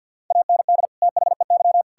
Morse code, heard as a keyed tone, about 50 wpm, 700 Hz. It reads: AND THEY
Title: Sentences from Top 1600 Words 50wpm